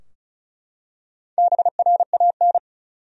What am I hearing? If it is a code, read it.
Morse code, heard as a keyed tone, about 36 wpm, 700 Hz. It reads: BRAN
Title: USA Common Names - Max 5 Letters - 35wpm